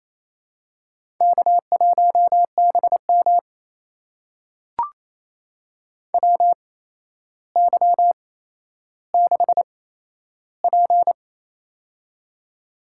Morse code, heard as a keyed tone, about 28 wpm, spaced as for 12 wpm, 700 Hz. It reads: K1BM WY6P
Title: USA Call Signs - 8x Character Spacing - 28wpm